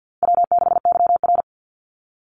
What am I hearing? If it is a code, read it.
Morse code, heard as a keyed tone, about 50 wpm, 700 Hz. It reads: W6YR
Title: Easy Call Signs Repeated 3x - 50wpm